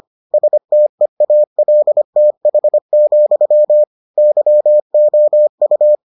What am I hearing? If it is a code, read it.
Morse code, heard as a keyed tone, about 25 wpm, 600 Hz. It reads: STEALTH, YOU